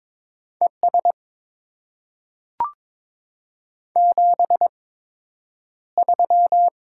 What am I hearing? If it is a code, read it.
Morse code, heard as a keyed tone, about 22 wpm, spaced as for 9 wpm, 700 Hz. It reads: ES 73
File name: Top 20 QSO Elements - 8x Character Spacing - 22wpm